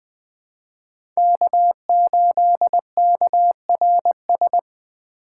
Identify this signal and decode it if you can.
Morse code, heard as a keyed tone, about 20 wpm, 700 Hz. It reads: K8KRS
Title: USA Call Signs - 5 Characters - 2x Character Spacing - 20wpm